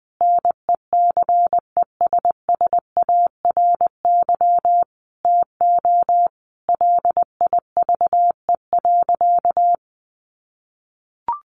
Morse code, heard as a keyed tone, about 20 wpm, 700 Hz. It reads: NECESSARY TO LIVE.
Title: Sentences from Top 800 Words 20wpm